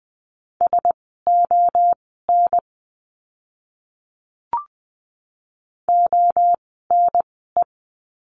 Morse code, heard as a keyed tone, about 20 wpm, spaced as for 15 wpm, 700 Hz. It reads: SON ONE
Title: 101 Lesson 17f - Letter u - Words 20wpm